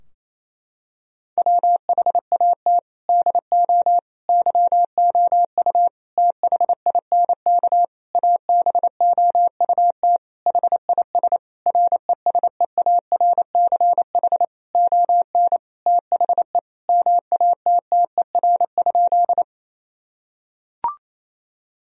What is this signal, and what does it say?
Morse code, heard as a keyed tone, about 28 wpm, 700 Hz. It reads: WHAT DO YOU THINK ABOUT HIS RESEARCH ON THE MATTER?
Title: Sentences from Top 1000 Words - Review Part 4 - 28wpm